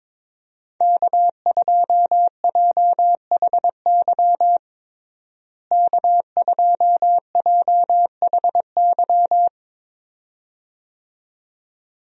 Morse code, heard as a keyed tone, about 22 wpm, 700 Hz. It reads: K2JHY K2JHY
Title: USA Call Signs - 5 Characters - Repeated 2x - 22wpm